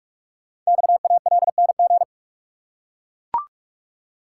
Morse code, heard as a keyed tone, about 44 wpm, 700 Hz. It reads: XALNG
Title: Code Groups - 5 Letters - 45wpm